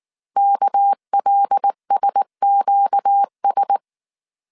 Morse code, heard as a keyed tone, about 19 wpm, 800 Hz. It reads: KLSQS
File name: CW / DET / 1 6 1500C S A 18wpm 2255